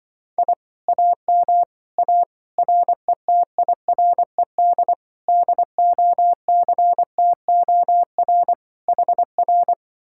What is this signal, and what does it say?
Morse code, heard as a keyed tone, about 23 wpm, 700 Hz. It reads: I AM A RETIRED DOCTOR HR
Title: Speed-Racing Common QSO Phrases - 20wpm